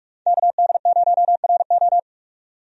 Morse code, heard as a keyed tone, about 44 wpm, 700 Hz. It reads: KD0RO
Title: USA Call Signs - 5 Characters - 45wpm